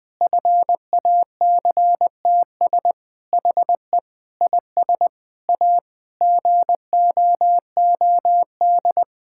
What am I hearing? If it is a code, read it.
Morse code, heard as a keyed tone, about 20 wpm, 700 Hz. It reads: FACTS HE IS A GOOD
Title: Sentences with 3-Word Phrases - Spoken Beginning - 20wpm